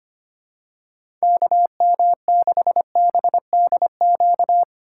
Morse code, heard as a keyed tone, about 25 wpm, 700 Hz. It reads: KM6BDQ
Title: USA Call Signs - 6 Characters - 4x Character Spacing - 25wpm